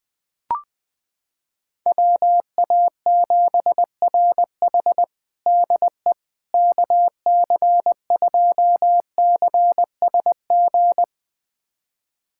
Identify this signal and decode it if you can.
Morse code, heard as a keyed tone, about 20 wpm, 700 Hz. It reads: WA7RH DE KC2CSG
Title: USA Callsign DE Callsign 20wpm